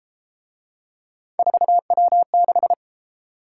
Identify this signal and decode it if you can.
Morse code, heard as a keyed tone, about 32 wpm, 700 Hz. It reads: 4W6